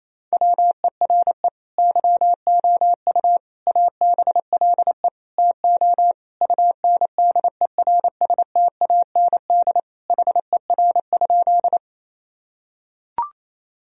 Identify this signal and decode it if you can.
Morse code, heard as a keyed tone, about 28 wpm, 700 Hz. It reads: WERE YOU ABLE TO UNDERSTAND HER?
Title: Sentences from Top 1000 Words - Review Part 2 - 28wpm